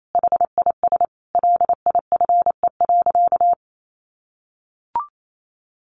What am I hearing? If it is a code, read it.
Morse code, heard as a keyed tone, about 28 wpm, 700 Hz. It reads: HIS LIFE.
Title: Sentences from Top 1100 Words 28wpm